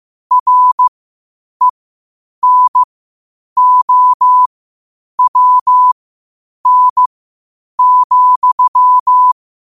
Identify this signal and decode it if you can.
Morse code, heard as a keyed tone, about 15 wpm, spaced as for 8 wpm, 1000 Hz. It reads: RENOWN,
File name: qotdfiles / 8wpm